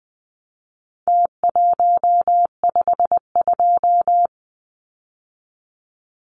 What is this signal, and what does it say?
Morse code, heard as a keyed tone, about 20 wpm, 700 Hz. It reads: T152
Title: Numbers - Contest Style - 4-Digit 20wpm